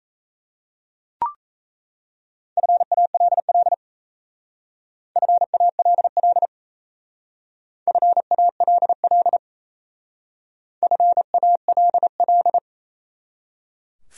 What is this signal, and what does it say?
Morse code, heard as a keyed tone, about 36 wpm, 700 Hz. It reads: FALL FALL FALL FALL